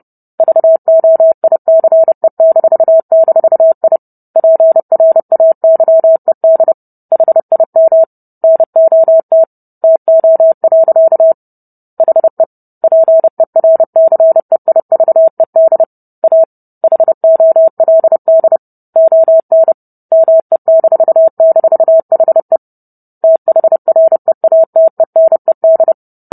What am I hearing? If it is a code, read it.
Morse code, heard as a keyed tone, about 30 wpm, 650 Hz. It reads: VOICE--I PRAYED HIM NOT TO. HE PERCEIVED A HOLD ON ME--HE THREATENED